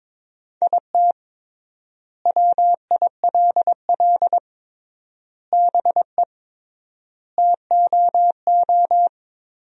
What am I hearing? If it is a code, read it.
Morse code, heard as a keyed tone, about 22 wpm, 700 Hz. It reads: IT WILL BE TOO
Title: 3x Word-Spacing - Sentences from Top 300 Words - 22wpm